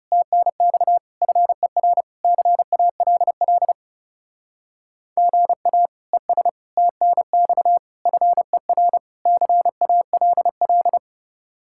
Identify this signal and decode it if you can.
Morse code, heard as a keyed tone, about 36 wpm, 700 Hz. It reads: TNX FER CALL GA ES TNX FER CALL
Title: Speed-Racing Common QSO Phrases - 30wpm